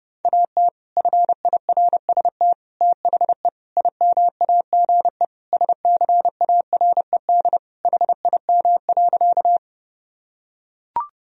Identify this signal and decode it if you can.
Morse code, heard as a keyed tone, about 31 wpm, 700 Hz. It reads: AT FIRST THE IMAGE SCARED HIM.
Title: Sentences from Top 1000 Words - Review Part 4 - 30wpm